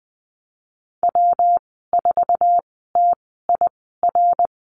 Morse code, heard as a keyed tone, about 19 wpm, spaced as for 14 wpm, 700 Hz. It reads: W4TIR
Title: 128 Lesson 21g - Number 6 - Call Signs 20wpm